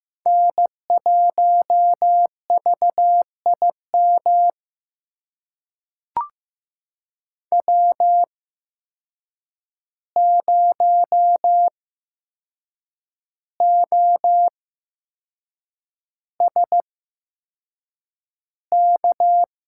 Morse code, heard as a keyed tone, about 15 wpm, spaced as for 6 wpm, 700 Hz. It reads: N1VIM W0OSK
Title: USA Call Signs - 5 Characters - 8x Character Spacing - 15wpm